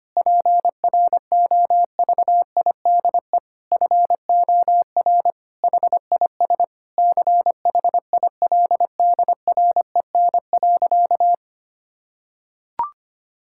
Morse code, heard as a keyed tone, about 25 wpm, 700 Hz. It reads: PROVIDE FOR HIS CHILDREN.
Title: Sentences from Top 1600 Words 25wpm